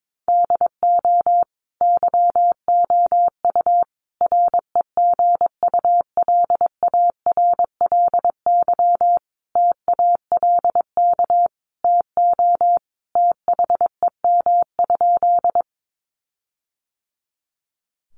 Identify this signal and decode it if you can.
Morse code, heard as a keyed tone, about 22 wpm, 700 Hz. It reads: DO YOU REGULARLY TALK TO THEM?
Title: Sentences from Top 1400 Words 22wpm